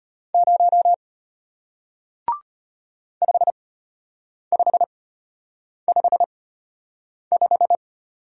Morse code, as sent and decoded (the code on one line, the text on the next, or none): ----- / ..... ..... ..... .....
0 5555